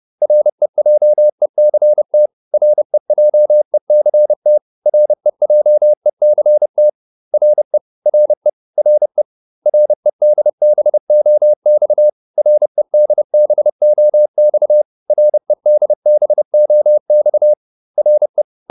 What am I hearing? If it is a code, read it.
Morse code, heard as a keyed tone, about 30 wpm, 600 Hz. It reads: REJECT REJECT REJECT RE RE RE REDBOX REDBOX REDBOX RE